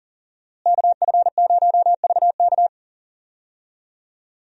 Morse code, heard as a keyed tone, about 41 wpm, 700 Hz. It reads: KF0VK
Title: USA Call Signs - 5 Characters - Repeated 3x - 40wpm